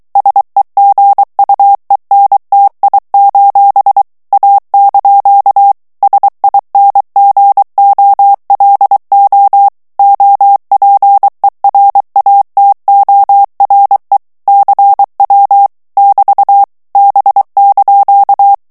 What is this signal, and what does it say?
Morse code, heard as a keyed tone, about 23 wpm, 800 Hz. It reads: SEGUENTI: A) SINGOLO OPERATORE CW <BT> B)
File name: ITALIEN texte clair 2 vitesse 22 mots minute